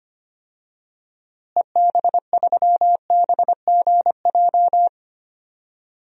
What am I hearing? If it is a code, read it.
Morse code, heard as a keyed tone, about 25 wpm, 700 Hz. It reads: EB3BGJ
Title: DXCC-100 Call Signs - 6 Characters - Repeated 3x - 25wpm